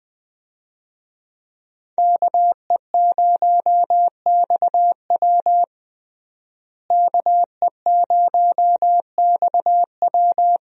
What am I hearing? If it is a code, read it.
Morse code, heard as a keyed tone, about 21 wpm, 700 Hz. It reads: KE0XW KE0XW
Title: USA Call Signs - 5 Characters - Repeated 2x - 20wpm